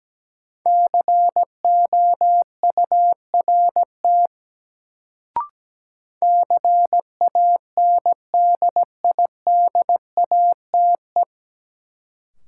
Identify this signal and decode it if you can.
Morse code, heard as a keyed tone, about 17 wpm, 700 Hz. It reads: COURT CANDIDATE
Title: Sentences from Top 1600 Words - New Word Drill Alt - 17wpm